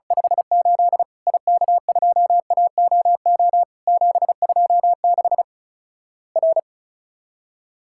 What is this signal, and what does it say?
Morse code, heard as a keyed tone, about 36 wpm, 700 Hz. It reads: 58 IK2AOO 726 R
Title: Worked All Europe DX Contest - QTC Exchange - 35wpm